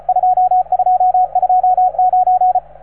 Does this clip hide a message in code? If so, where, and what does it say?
Yes: Morse code from 0.1 to 2.6 s, heard as a keyed tone, about 36 wpm, 750 Hz, with radio noise. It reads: 2229